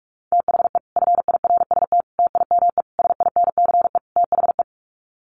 Morse code, heard as a keyed tone, about 44 wpm, 700 Hz. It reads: THE FIRST TIME SINCE THE
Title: Sets of 5 Words Encore 45wpm